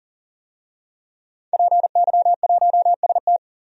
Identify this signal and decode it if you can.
Morse code, heard as a keyed tone, about 40 wpm, 700 Hz. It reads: PY1ST
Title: DXCC-100 Call Signs - 5 Characters - 40wpm